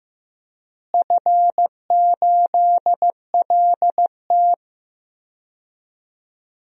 Morse code, heard as a keyed tone, about 15 wpm, 700 Hz. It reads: F8LT
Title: DXCC-100 Call Signs - 4 Characters - Repeated 2x - 15wpm